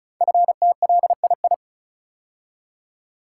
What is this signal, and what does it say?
Morse code, heard as a keyed tone, about 36 wpm, 700 Hz. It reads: FTLII
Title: Code Groups - 5 Letters - 35wpm